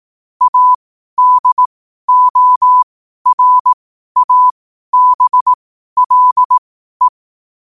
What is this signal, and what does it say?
Morse code, heard as a keyed tone, about 18 wpm, spaced as for 12 wpm, 1000 Hz. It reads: ADORABLE
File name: qotdfiles / 12wpm